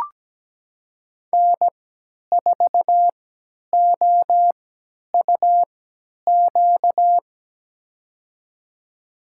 Morse code, heard as a keyed tone, about 17 wpm, spaced as for 10 wpm, 700 Hz. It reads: N4OUQ
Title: USA Call Signs - 5 Characters - 3x Character Spacing - 17wpm